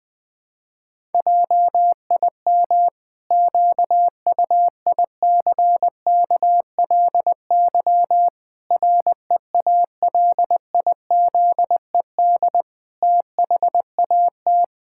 Morse code, heard as a keyed tone, about 21 wpm, 700 Hz. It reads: JIM QUICKLY REALIZED THAT
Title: Pangram Challenge v2 - 20wpm